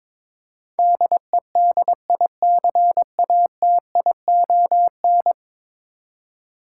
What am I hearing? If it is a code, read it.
Morse code, heard as a keyed tone, about 22 wpm, 700 Hz. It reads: DEDICATION
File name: practice / 22wpm / 10 Letter Words - First 3 Letters Given - 22wpm